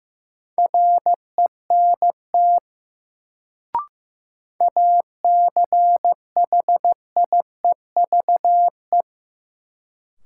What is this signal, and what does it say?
Morse code, heard as a keyed tone, about 15 wpm, 700 Hz. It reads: RENT ACHIEVE
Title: Sentences from Top 1000 Words - Review Part 5 - New Word Drill - 15wpm